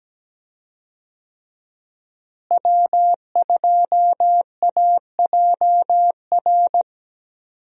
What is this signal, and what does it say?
Morse code, heard as a keyed tone, about 17 wpm, 700 Hz. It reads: W2AJR
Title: USA Call Signs - 5 Characters - Repeated 3x - 17wpm